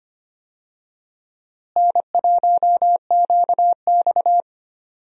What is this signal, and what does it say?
Morse code, heard as a keyed tone, about 25 wpm, 700 Hz. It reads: N1QX
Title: USA Call Signs - 4 Characters - Repeated 3x - 25wpm